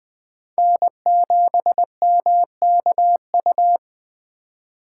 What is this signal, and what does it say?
Morse code, heard as a keyed tone, about 21 wpm, 700 Hz. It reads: N7MKU